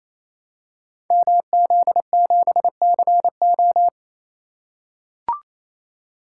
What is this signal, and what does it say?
Morse code, heard as a keyed tone, about 27 wpm, 700 Hz. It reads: MZ7CO
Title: Code Groups - 5 Letter:Numbers - 28wpm